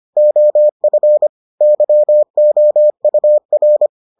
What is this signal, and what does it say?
Morse code, heard as a keyed tone, about 25 wpm, 600 Hz. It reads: OF YOUR